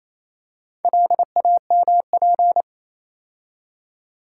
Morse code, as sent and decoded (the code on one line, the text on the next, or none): .-.. .- -- .--.
LAMP